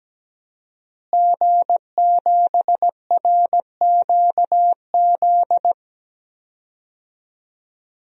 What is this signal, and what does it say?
Morse code, heard as a keyed tone, about 17 wpm, 700 Hz. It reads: G7RQZ
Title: DXCC-100 Call Signs - 5 Characters - Repeated 2x - 17wpm